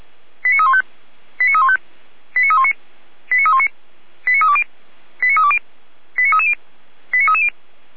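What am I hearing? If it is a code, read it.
Five-tone selective call (ZVEI), heard as quick runs of five tones, tones of about 70 ms. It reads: 89315 89315 89319 89319 89320 89320 89330 89330
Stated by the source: Received on an SDR; ZVEI five-tone alert from the server of Feuerwehr Orlamünde (f85360).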